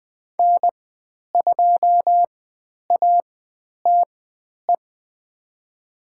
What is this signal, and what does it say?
Morse code, heard as a keyed tone, about 20 wpm, spaced as for 10 wpm, 700 Hz. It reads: N2ATE
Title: 095 Lesson 16g - Letter c - Call Signs 20wpm